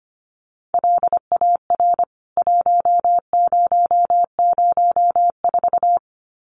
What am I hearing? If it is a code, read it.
Morse code, heard as a keyed tone, about 25 wpm, 700 Hz. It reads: LAR 1004
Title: CWT Exchange 25wpm